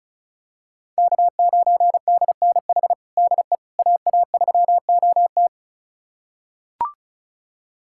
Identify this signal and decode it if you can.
Morse code, heard as a keyed tone, about 36 wpm, 700 Hz. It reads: K9DNH DE AA3OT
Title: USA Callsign DE Callsign - 5 Characters - 35wpm